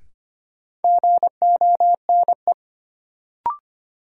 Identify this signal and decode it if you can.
Morse code, heard as a keyed tone, about 25 wpm, 700 Hz. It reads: GONE
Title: Sentences from Top 1200 Words - New Word Drill Alt - 25wpm